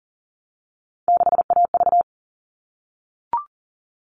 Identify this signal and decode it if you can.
Morse code, heard as a keyed tone, about 38 wpm, 700 Hz. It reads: 6AV
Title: All Cut Numbers - 3-Digit 40wpm